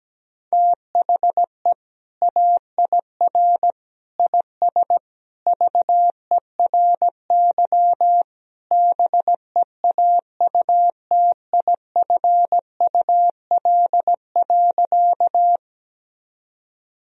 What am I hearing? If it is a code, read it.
Morse code, heard as a keyed tone, about 17 wpm, 700 Hz. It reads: THE AIR IS VERY BEAUTIFUL.
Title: Sentences from Top 1000 Words - Review Part 1 - 17wpm